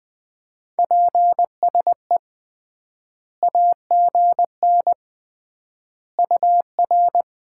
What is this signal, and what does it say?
Morse code, heard as a keyed tone, about 20 wpm, 700 Hz. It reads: PSE AGN UR
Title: Common QSO Phrases - 3x Word-Spacing - 20wpm